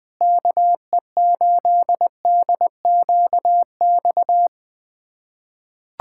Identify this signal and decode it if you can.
Morse code, heard as a keyed tone, about 20 wpm, 700 Hz. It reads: KE8DQX
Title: USA Callsign DE Callsign 20wpm